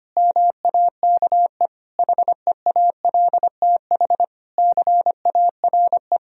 Morse code, heard as a keyed tone, about 25 wpm, 700 Hz. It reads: MAKE HEALTH CARE
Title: Sentences from Top 1000 Words - Review Part 3 - 25wpm